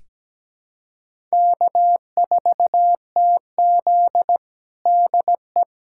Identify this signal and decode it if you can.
Morse code, heard as a keyed tone, about 17 wpm, 700 Hz. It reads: K4TZ DE
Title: USA Callsign DE Callsign - 4 Characters - 17wpm